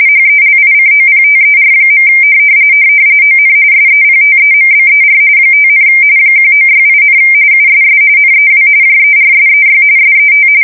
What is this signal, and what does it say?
RTTY, about 45.45 baud. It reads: HELICOPTER. ON SEPTEMBER 5, 1968, USS NEW JERSEY SET
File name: fsk170 / battleshipnj